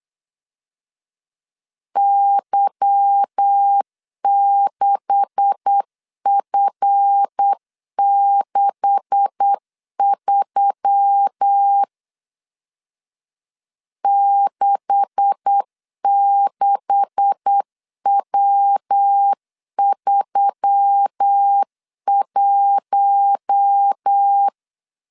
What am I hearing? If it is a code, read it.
Morse code, heard as a keyed tone, about 8 wpm, 800 Hz. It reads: Y6F63 66W31
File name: LEZIONI / 9 286 8wpm 3425